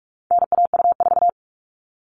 Morse code, heard as a keyed tone, about 44 wpm, 700 Hz. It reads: NAU4
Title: All Cut Numbers - 4-Digit 45wpm